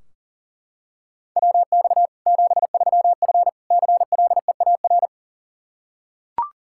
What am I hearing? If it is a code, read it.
Morse code, heard as a keyed tone, about 40 wpm, 700 Hz. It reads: WX 73F CLEAR
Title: Practice QSO - Exchange 2 - Info - 40wpm